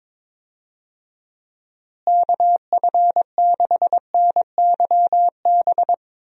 Morse code, heard as a keyed tone, about 22 wpm, 700 Hz. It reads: KF6NYB
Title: USA Call Signs - 6 Characters - Repeated 3x - 22wpm